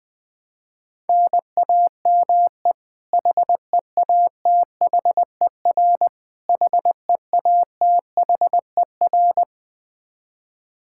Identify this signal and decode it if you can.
Morse code, heard as a keyed tone, about 21 wpm, 700 Hz. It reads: NAME HEATHER HEATHER
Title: K1USN SST QSO Practice 20wpm